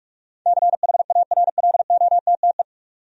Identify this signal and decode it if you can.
Morse code, heard as a keyed tone, about 42 wpm, 700 Hz. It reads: CHARLOTTE
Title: USA Common Names - 45wpm